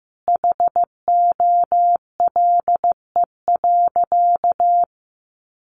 Morse code, heard as a keyed tone, about 15 wpm, 700 Hz. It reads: HOLE.
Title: Easy Phrases 15wpm